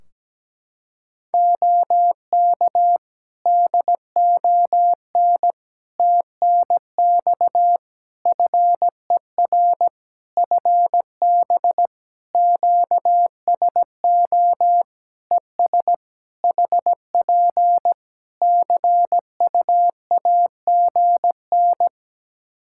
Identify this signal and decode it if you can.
Morse code, heard as a keyed tone, about 17 wpm, 700 Hz. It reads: OK DON TNX FER FB QSO ES HP CUAGN
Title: Practice QSO - Exchange 4 - Ending - 17wpm